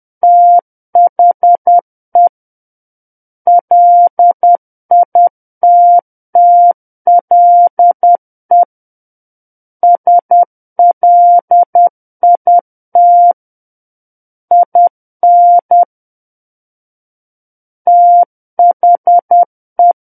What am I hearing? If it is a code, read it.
Morse code, heard as a keyed tone, about 10 wpm, 700 Hz. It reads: THE LITTLE SLIT IN THE
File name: Book One / 01-Chapter 1 - 10 WPM